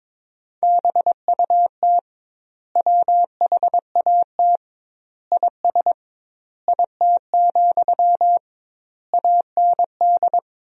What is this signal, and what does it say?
Morse code, heard as a keyed tone, about 22 wpm, 700 Hz. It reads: BUT WHAT IS IT, AND
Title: 2x Word-Spacing - Sentences from Top 300 Words - 22wpm